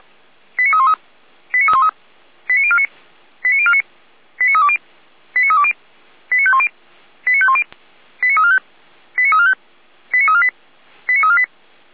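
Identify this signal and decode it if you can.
Five-tone selective call (ZVEI), heard as quick runs of five tones, tones of about 70 ms. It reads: 89313 89313 89049 89049 89320 89320 89510 89510 89345 89345 89348 89348
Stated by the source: Five-tone ZVEI selective call received on an SDR.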